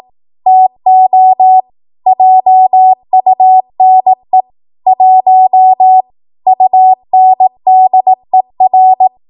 Morse code, heard as a keyed tone, about 18 wpm, 750 Hz. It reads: TO JUNE 1 UNDER
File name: Morse / 18WPM